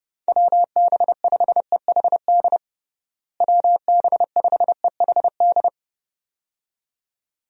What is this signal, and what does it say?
Morse code, heard as a keyed tone, about 32 wpm, 700 Hz. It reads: WB5EHD WB5EHD
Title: USA Call Signs - 6 Characters - Repeated 2x - 30wpm